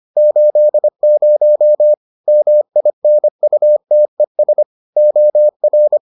Morse code, heard as a keyed tone, about 25 wpm, 600 Hz. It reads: 80 MINUTES OR